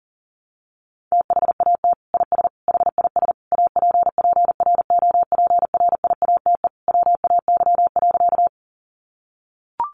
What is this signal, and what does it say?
Morse code, heard as a keyed tone, about 40 wpm, 700 Hz. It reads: THAT IS HIS APPROPRIATE WAY.
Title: Sentences from Top 900 Words 40wpm